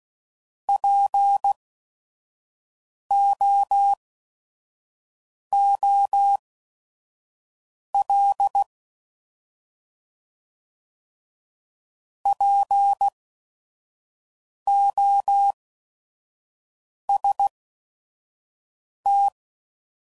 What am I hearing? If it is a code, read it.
Morse code, heard as a keyed tone, about 16 wpm, spaced as for 5 wpm, 800 Hz. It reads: POOL POST
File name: Prefixes / By First Letter / Ps